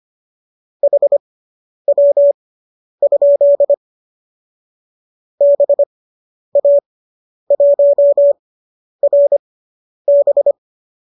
Morse code, heard as a keyed tone, about 25 wpm, spaced as for 10 wpm, 570 Hz. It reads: HW? BA1RB